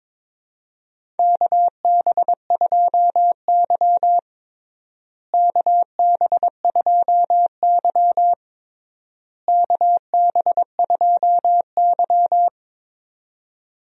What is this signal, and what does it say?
Morse code, heard as a keyed tone, about 22 wpm, 700 Hz. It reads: KB2Y KB2Y KB2Y
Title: USA Call Signs - 4 Characters - Repeated 3x - 22wpm